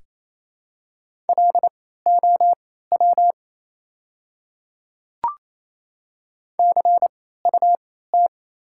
Morse code, heard as a keyed tone, about 28 wpm, spaced as for 16 wpm, 700 Hz. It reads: LOW CUT
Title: 3x Character Spacing - Top 300 Words - 28wpm